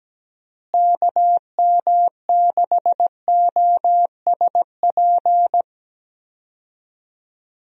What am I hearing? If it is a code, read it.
Morse code, heard as a keyed tone, about 17 wpm, 700 Hz. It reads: KM6OSP
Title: USA Call Signs - 6 Characters - Repeated 2x - 17wpm